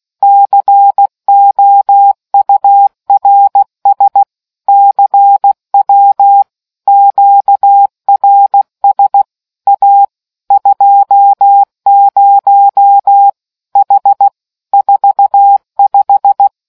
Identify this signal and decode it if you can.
Morse code, heard as a keyed tone, about 16 wpm, 800 Hz. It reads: COURS CW QRS A 20 H 45